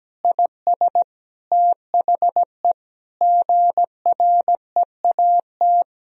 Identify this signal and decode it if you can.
Morse code, heard as a keyed tone, about 17 wpm, 700 Hz. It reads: IS THE GREAT
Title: Sentences from Top 1000 Words - Review Part 2 - 17wpm